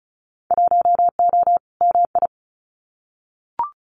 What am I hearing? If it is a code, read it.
Morse code, heard as a keyed tone, about 36 wpm, 700 Hz. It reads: 1O MI
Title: Winter Field Day Exchanges Encore 35wpm